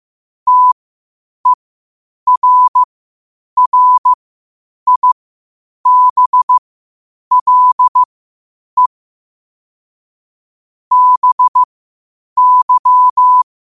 Morse code, heard as a keyed tone, about 15 wpm, spaced as for 8 wpm, 1000 Hz. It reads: TERRIBLE BY
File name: qotdfiles / 8wpm